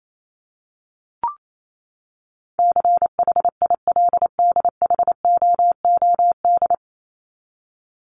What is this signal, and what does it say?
Morse code, heard as a keyed tone, about 28 wpm, 700 Hz. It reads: CHILDHOOD
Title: Top 1751-2000 Words 28wpm